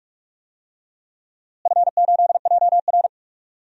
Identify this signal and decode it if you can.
Morse code, heard as a keyed tone, about 46 wpm, 700 Hz. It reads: F8JR
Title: DXCC-100 Call Signs - 4 Characters - Repeated 3x - 45wpm